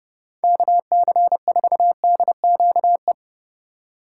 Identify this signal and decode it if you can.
Morse code, heard as a keyed tone, about 36 wpm, 700 Hz. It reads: KC4DQE
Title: Speed Racing - USA 2x3 Call Signs 30wpm